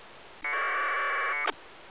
Packet radio (AX.25, as in APRS).